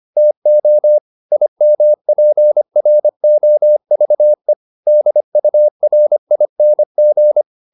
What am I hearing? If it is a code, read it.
Morse code, heard as a keyed tone, about 25 wpm, 600 Hz. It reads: TO IMPROVE DURING
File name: SS211 25